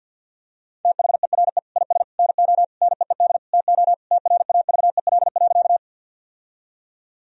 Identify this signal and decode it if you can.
Morse code, heard as a keyed tone, about 50 wpm, 700 Hz. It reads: THERE IS NO NEED TO TRAVEL.
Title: Sentences from Top 1000 Words - Review Part 2 - 50wpm